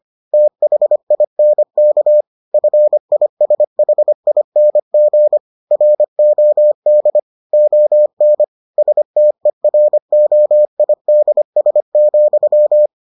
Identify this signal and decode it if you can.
Morse code, heard as a keyed tone, about 25 wpm, 600 Hz. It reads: THINK FISHING ROD ON STEROIDS,